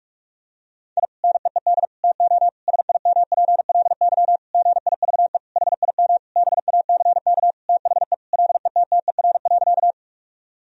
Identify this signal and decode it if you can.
Morse code, heard as a keyed tone, about 41 wpm, 700 Hz. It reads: I NEED TO SIMPLY GIVE HIM BACK THE LETTER.
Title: Sentences from Top 1000 Words - Review Part 4 - 45wpm